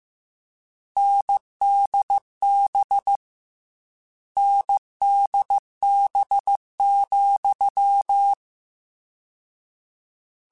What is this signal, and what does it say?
Morse code, heard as a keyed tone, about 15 wpm, 800 Hz. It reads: NDB NDB,
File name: Difficult Combos / Difficult Combos-mp3s / NDB NDB6 @13